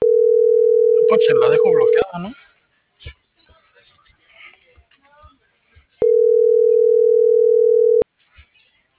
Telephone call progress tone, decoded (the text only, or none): ringback tone